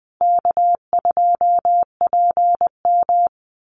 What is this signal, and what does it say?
Morse code, heard as a keyed tone, about 20 wpm, 700 Hz. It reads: K2PM
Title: USA Call Signs Repeated 3x 20wpm